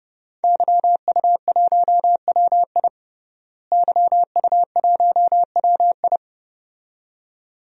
Morse code, heard as a keyed tone, about 29 wpm, 700 Hz. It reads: YU1WI YU1WI
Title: DXCC-100 Call Signs - 5 Characters - Repeated 2x - 30wpm